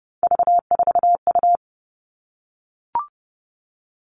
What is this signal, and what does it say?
Morse code, heard as a keyed tone, about 29 wpm, 700 Hz. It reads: V4U
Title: All Cut Numbers - 3-Digit 30wpm